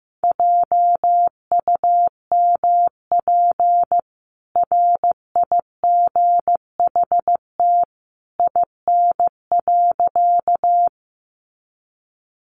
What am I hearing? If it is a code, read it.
Morse code, heard as a keyed tone, about 15 wpm, 700 Hz. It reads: JUMP RIGHT IN.
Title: Sentences from Top 1300 Words 15wpm